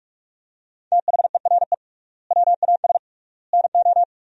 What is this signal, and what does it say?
Morse code, heard as a keyed tone, about 46 wpm, 700 Hz. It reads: THERE WAS NO
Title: 3x Word-Spacing - Sentences from Top 300 Words - 45wpm